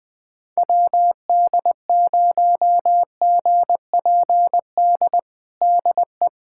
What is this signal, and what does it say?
Morse code, heard as a keyed tone, about 21 wpm, 700 Hz. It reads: WD0GPD DE
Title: USA Callsign DE Callsign 20wpm